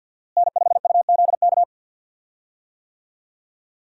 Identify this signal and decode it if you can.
Morse code, heard as a keyed tone, about 50 wpm, 700 Hz. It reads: N5UZK
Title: USA Call Signs - 5 Characters - Repeated 2x - 50wpm